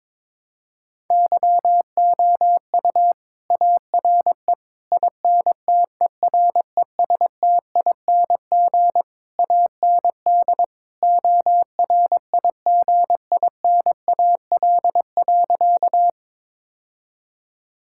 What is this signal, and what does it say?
Morse code, heard as a keyed tone, about 22 wpm, 700 Hz. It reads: YOU ARE INTERESTING AND ORIGINAL.
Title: Sentences from Top 1000 Words - Review Part 6 - 22wpm